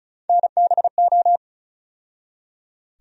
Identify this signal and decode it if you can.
Morse code, heard as a keyed tone, about 36 wpm, 700 Hz. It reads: NBO
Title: USA Call Sign Postfixes - 3 Letters 35wpm